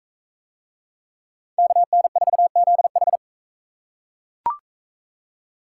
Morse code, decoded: KN4ZH

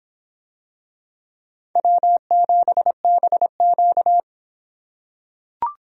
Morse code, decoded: W7BQ

